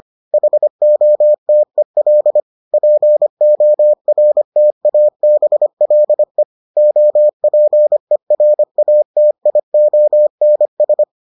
Morse code, heard as a keyed tone, about 25 wpm, 600 Hz. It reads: HOTEL PORTABLE OPERATIONS